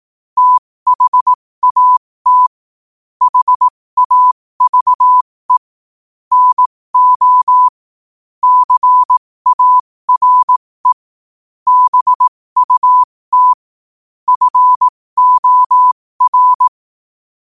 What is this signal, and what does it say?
Morse code, heard as a keyed tone, about 18 wpm, spaced as for 15 wpm, 1000 Hz. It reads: THAT HAVE NO CARE BUT FOR
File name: qotdfiles / 15wpm